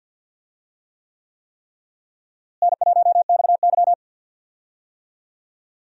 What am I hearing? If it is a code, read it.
Morse code, heard as a keyed tone, about 51 wpm, 700 Hz. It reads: N1XY
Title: USA Call Signs - 4 Characters - Repeated 3x - 50wpm